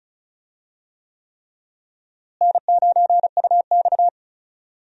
Morse code, heard as a keyed tone, about 36 wpm, 700 Hz. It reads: N9UX
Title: USA Call Signs - 4 Characters - Repeated 3x - 35wpm